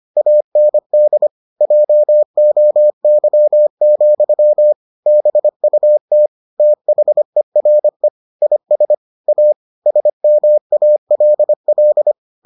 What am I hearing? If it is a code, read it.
Morse code, heard as a keyed tone, about 25 wpm, 600 Hz. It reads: AND JOY, BUT THERE IS A SMALL